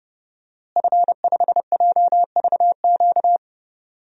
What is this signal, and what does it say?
Morse code, heard as a keyed tone, about 30 wpm, 700 Hz. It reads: F5JVQ
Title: DXCC-100 Call Signs - 5 Characters - Repeated 2x - 30wpm